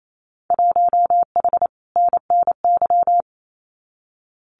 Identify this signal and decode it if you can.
Morse code, heard as a keyed tone, about 28 wpm, 700 Hz. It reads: 1H NNY
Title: Winter Field Day Exchanges 28wpm